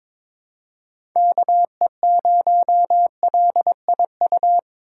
Morse code, heard as a keyed tone, about 22 wpm, 700 Hz. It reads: KE0LIU
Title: USA Call Signs - 6 Characters - 2x Character Spacing - 22wpm